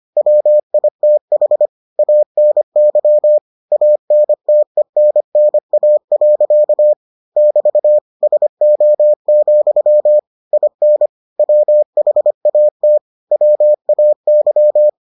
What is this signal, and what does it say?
Morse code, heard as a keyed tone, about 25 wpm, 600 Hz. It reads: WITH ANY ANTENNA. <BT> SO, IN WHAT WAY